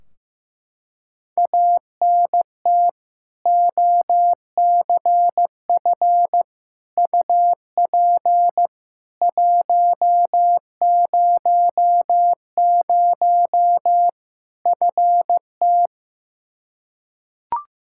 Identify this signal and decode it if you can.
Morse code, heard as a keyed tone, about 16 wpm, 700 Hz. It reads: ANT OCF UP 100 FT
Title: Practice QSO - Exchange 2 - Info - 15wpm